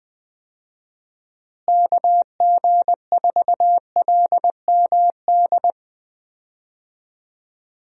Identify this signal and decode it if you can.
Morse code, heard as a keyed tone, about 21 wpm, 700 Hz. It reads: KG4LMD
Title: USA Call Signs - 6 Characters - Repeated 2x - 20wpm